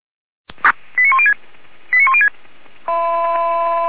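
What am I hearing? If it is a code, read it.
Five-tone selective call (ZVEI), heard as quick runs of five tones, tones of about 70 ms. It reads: 89106 89106
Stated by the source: ZVEI five-tone selective call 89105 received on SDR.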